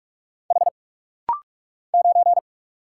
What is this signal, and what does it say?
Morse code, heard as a keyed tone, about 44 wpm, 700 Hz. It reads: H 9
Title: Single Letter-Number - Rapid-Fire 45wpm